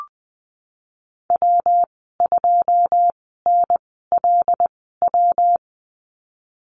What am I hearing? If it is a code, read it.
Morse code, heard as a keyed tone, about 20 wpm, spaced as for 15 wpm, 700 Hz. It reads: W2NLW
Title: 128 Lesson 21g - Number 6 - Call Signs 20wpm